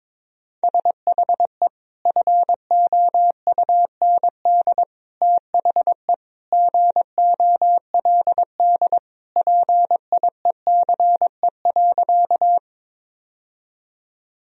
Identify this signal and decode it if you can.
Morse code, heard as a keyed tone, about 22 wpm, 700 Hz. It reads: SHE FOUND THE GOLD PIECE.
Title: Sentences from Top 1000 Words - Review Part 3 - 22wpm